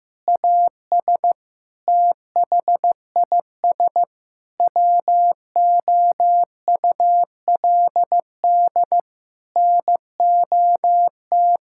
Morse code, heard as a keyed tone, about 15 wpm, 700 Hz. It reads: AS THIS WOULD NOT
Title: Sentences from Top 1600 Words 15wpm